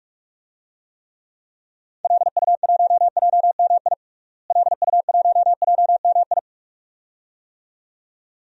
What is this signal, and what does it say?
Morse code, heard as a keyed tone, about 46 wpm, 700 Hz. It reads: LU1JMI LU1JMI
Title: DXCC-100 Call Signs - 6 Characters - Repeated 2x - 45wpm